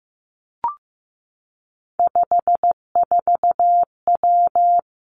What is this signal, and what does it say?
Morse code, heard as a keyed tone, about 15 wpm, 700 Hz. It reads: 54W
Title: Three Challenging Letter-Number Combinations 15wpm